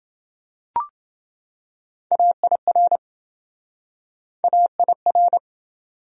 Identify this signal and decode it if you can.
Morse code, heard as a keyed tone, about 26 wpm, 700 Hz. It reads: AIR AIR